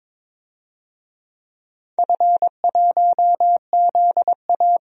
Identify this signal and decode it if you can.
Morse code, heard as a keyed tone, about 22 wpm, 700 Hz. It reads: F1ZA
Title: DXCC-100 Call Signs - 4 Characters - 22wpm